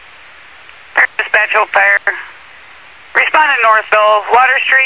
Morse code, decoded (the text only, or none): none